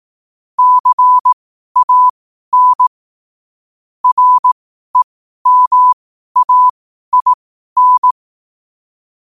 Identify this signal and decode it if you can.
Morse code, heard as a keyed tone, about 18 wpm, spaced as for 12 wpm, 1000 Hz. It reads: CAN REMAIN
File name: qotdfiles / 12wpm